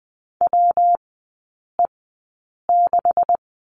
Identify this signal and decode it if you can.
Morse code, heard as a keyed tone, about 20 wpm, spaced as for 8 wpm, 700 Hz. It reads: WE6
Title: 191 Lesson 31e - Letter v - Three Characters 20wpm